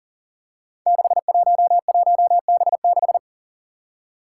Morse code, heard as a keyed tone, about 40 wpm, 690 Hz. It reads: 611B6
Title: Five Challenging Letter-Number Combinations 40wpm